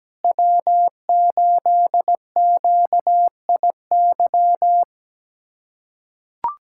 Morse code, heard as a keyed tone, about 17 wpm, 700 Hz. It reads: W8QIY
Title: USA Callsign DE Callsign - 5 Characters - 17wpm